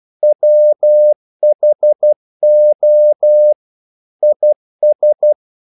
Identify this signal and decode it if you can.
Morse code, heard as a keyed tone, about 12 wpm, 600 Hz. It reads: WHO IS
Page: English quotations in CW (telegraphy - morse)